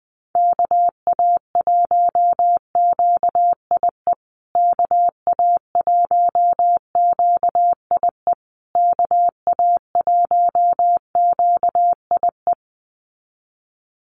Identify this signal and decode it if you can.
Morse code, heard as a keyed tone, about 20 wpm, 700 Hz. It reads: KA1QIE KA1QIE KA1QIE
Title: USA Call Signs Repeated 3x 20wpm